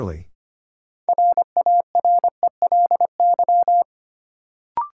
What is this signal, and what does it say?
Morse code, heard as a keyed tone, about 25 wpm, 700 Hz. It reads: RARELY
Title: Sentences from Top 1400 Words - New Word Drill - 25wpm